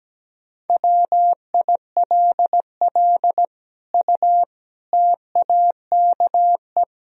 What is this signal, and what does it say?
Morse code, heard as a keyed tone, about 17 wpm, 700 Hz. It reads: WILL U TAKE
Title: Sentences from Top 200 Words and Abbreviations 17wpm